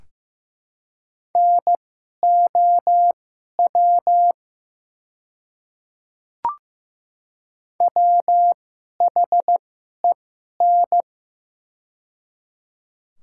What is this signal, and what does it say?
Morse code, heard as a keyed tone, about 15 wpm, spaced as for 11 wpm, 700 Hz. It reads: NOW WHEN
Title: 2x Character Spacing - Top 300 Words - 15wpm